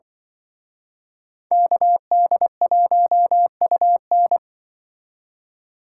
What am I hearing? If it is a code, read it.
Morse code, heard as a keyed tone, about 24 wpm, 700 Hz. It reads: KD1UN
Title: Speed Racing - USA 2x2 Call Signs 20wpm